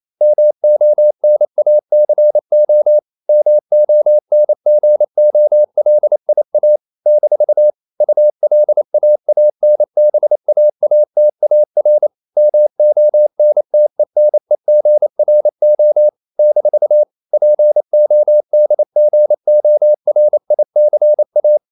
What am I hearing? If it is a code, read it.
Morse code, heard as a keyed tone, about 28 wpm, 600 Hz. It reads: MONACO MONGOLIA - ULAANBAATAR MONTENEGRO - PODGORICA
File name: CW / Practice / NationalCapitals28